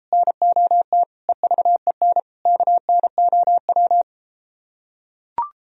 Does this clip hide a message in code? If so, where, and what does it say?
Yes: Morse code from 0.1 to 4.0 s, heard as a keyed tone, about 33 wpm, 700 Hz. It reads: NOT EVEN KNOW